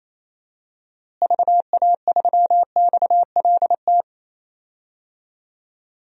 Morse code, heard as a keyed tone, about 28 wpm, 700 Hz. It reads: VA3XLT
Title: DXCC-100 Call Signs - 6 Characters - Repeated 3x - 28wpm